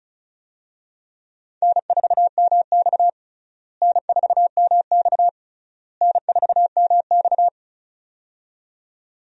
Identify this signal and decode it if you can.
Morse code, heard as a keyed tone, about 36 wpm, 700 Hz. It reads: N4MX N4MX N4MX
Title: USA Call Signs - 4 Characters - Repeated 3x - 35wpm